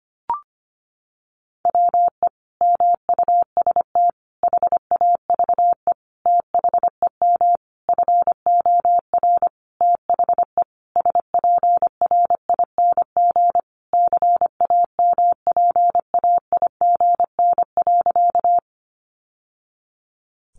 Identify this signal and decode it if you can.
Morse code, heard as a keyed tone, about 25 wpm, 700 Hz. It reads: WE MUST HAVE THEM FOR THE SPRING CAMPAIGN.
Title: Sentences from Top 1200 Words 25wpm